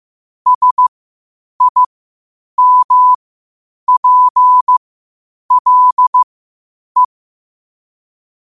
Morse code, heard as a keyed tone, about 15 wpm, spaced as for 8 wpm, 1000 Hz. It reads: SIMPLE
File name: qotdfiles / 8wpm